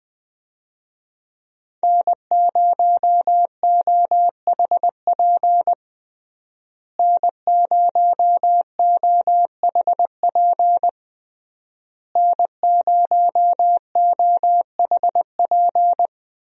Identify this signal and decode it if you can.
Morse code, heard as a keyed tone, about 20 wpm, 700 Hz. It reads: N0OHP N0OHP N0OHP
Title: USA Call Signs - 5 Characters - Repeated 3x - 20wpm